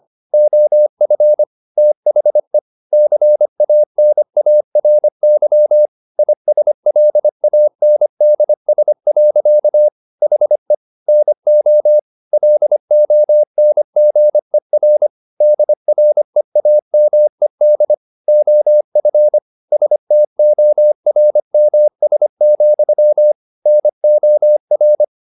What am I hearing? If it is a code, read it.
Morse code, heard as a keyed tone, about 25 wpm, 600 Hz. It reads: OF THE CANARY ISLANDS. HE NO LONGER DREAMED OF STORMS, NOR
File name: the old man and the sea part 1 25wpm